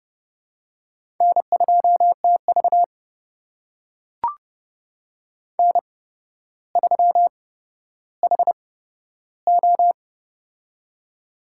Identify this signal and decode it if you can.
Morse code, heard as a keyed tone, about 29 wpm, spaced as for 13 wpm, 700 Hz. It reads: N2TV N3HO
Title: USA Call Signs - 4 Characters - 8x Character Spacing - 30wpm